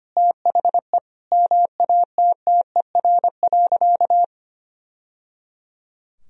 Morse code, heard as a keyed tone, about 25 wpm, 700 Hz. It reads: THE MATTER.
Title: Sentences from Top 1000 Words - Review Part 3 - 25wpm